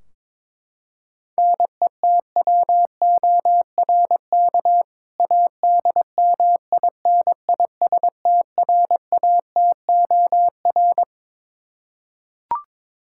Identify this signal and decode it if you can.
Morse code, heard as a keyed tone, about 22 wpm, 700 Hz. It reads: NETWORK ADMINISTRATOR
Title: K1USN SST QSO Practice 22wpm